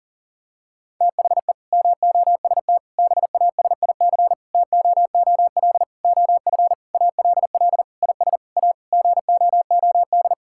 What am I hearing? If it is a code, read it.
Morse code, heard as a keyed tone, about 40 wpm, 700 Hz. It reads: THE MOST BASIC TOOL OF ALL IS A GOOD
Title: Sentences from Top 1000 Words - Review Part 6 - 40wpm